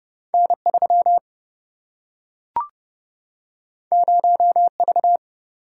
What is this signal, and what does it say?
Morse code, heard as a keyed tone, about 30 wpm, 700 Hz. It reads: N 3 0 V
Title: Single-Letter Single-Number 30wpm